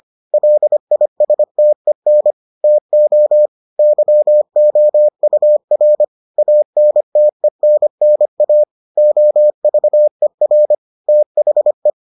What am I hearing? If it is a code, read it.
Morse code, heard as a keyed tone, about 25 wpm, 600 Hz. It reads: LISTEN TO YOUR ANTENNA OVER THE